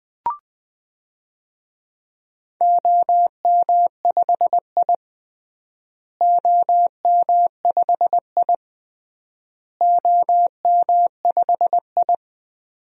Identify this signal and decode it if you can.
Morse code, heard as a keyed tone, about 20 wpm, 700 Hz. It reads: OM5I OM5I OM5I
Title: DXCC-100 Call Signs - 4 Characters - Repeated 3x - 20wpm